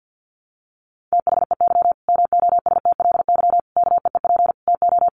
Morse code, heard as a keyed tone, about 50 wpm, 700 Hz. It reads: THEY MOSTLY KEEP TO
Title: Sentences from Top 1300 Words 50wpm